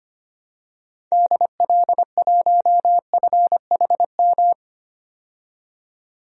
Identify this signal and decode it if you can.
Morse code, heard as a keyed tone, about 25 wpm, 700 Hz. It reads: DL1FHM
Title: DXCC-100 Call Signs - 6 Characters - 25wpm